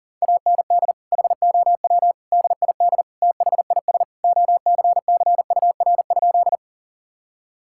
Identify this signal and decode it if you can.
Morse code, heard as a keyed tone, about 37 wpm, 700 Hz. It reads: AND HOW DID THIS OCCUR?
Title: Sentences from Top 1000 Words - Review Part 6 - 40wpm